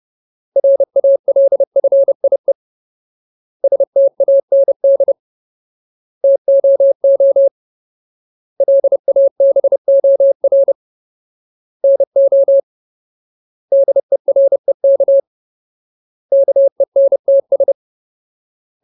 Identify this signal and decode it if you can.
Morse code, heard as a keyed tone, about 30 wpm, 550 Hz. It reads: RALFIE STAND TOO LABOR NO DEREK KENTS
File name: Ultimate Word List / 30WPM Ultimate